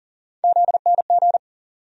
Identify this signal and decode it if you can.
Morse code, heard as a keyed tone, about 40 wpm, 700 Hz. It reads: ZNG
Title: USA Call Sign Postfixes - 3 Letters 40wpm